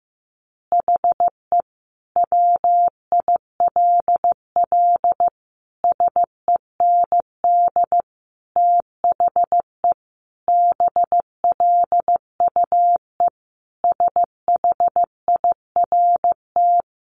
Morse code, heard as a keyed tone, about 15 wpm, 700 Hz. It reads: HE WILL SEND THE BLUE SHIRT
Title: Additional Easy Phrases 15wpm